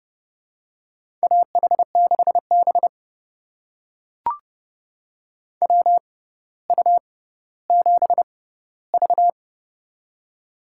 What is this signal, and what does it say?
Morse code, heard as a keyed tone, about 30 wpm, spaced as for 15 wpm, 700 Hz. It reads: AH6B WU7V
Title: USA Call Signs - 4 Characters - 6x Character Spacing - 30wpm